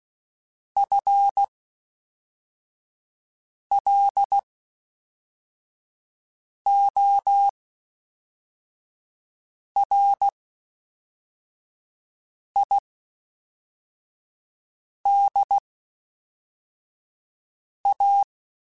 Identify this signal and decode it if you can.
Morse code, heard as a keyed tone, about 16 wpm, spaced as for 4 wpm, 800 Hz. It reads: FLORIDA
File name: Spaced / FL Runs sp mp3s / FL Runs sp @5